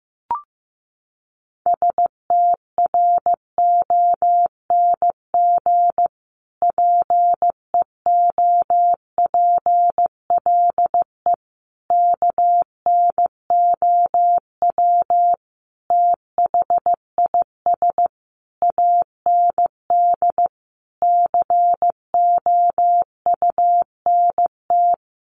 Morse code, heard as a keyed tone, about 15 wpm, 700 Hz. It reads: STRONG PEOPLE KNOW THIS AND COUNT
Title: Sentences from Top 1500 Words 15wpm